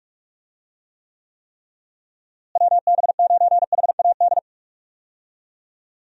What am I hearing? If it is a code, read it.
Morse code, heard as a keyed tone, about 46 wpm, 700 Hz. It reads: WB9HAD
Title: USA Call Signs - 6 Characters - Repeated 3x - 45wpm